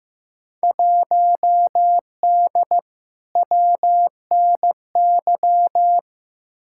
Morse code, heard as a keyed tone, about 15 wpm, 700 Hz. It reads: 1D WNY
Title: Field Day Exchanges - 15wpm